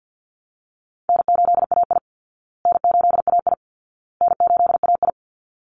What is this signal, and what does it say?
Morse code, heard as a keyed tone, about 50 wpm, 700 Hz. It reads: N8AI N8AI N8AI
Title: Easy Call Signs Repeated 3x - 50wpm